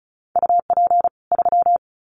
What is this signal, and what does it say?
Morse code, heard as a keyed tone, about 34 wpm, 700 Hz. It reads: UP 3
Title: Common QSO Phrases 35wpm